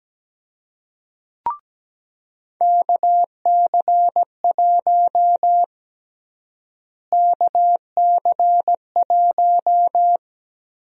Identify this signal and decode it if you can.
Morse code, heard as a keyed tone, about 17 wpm, 700 Hz. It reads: KC1 KC1
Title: USA Call Sign Prefixes - Repeated 2x - 17wpm